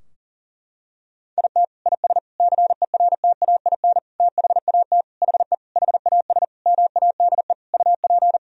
Morse code, heard as a keyed tone, about 40 wpm, 700 Hz. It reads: IT IS CERTAIN THAT HE HAS MADE UP
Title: Sentences from Top 1000 Words - Review Part 2 - 40wpm